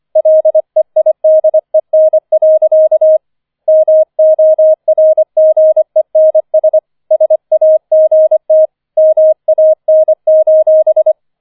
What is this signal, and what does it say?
Morse code, heard as a keyed tone, about 25 wpm, 630 Hz. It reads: LEIDEN. MORGENS SAGT MAN:
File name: 005 Fünfter Streich 25wpm